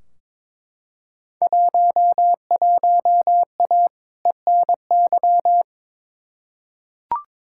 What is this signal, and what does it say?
Morse code, heard as a keyed tone, about 23 wpm, 700 Hz. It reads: 11A ENY
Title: Field Day Exchanges Encore - 22wpm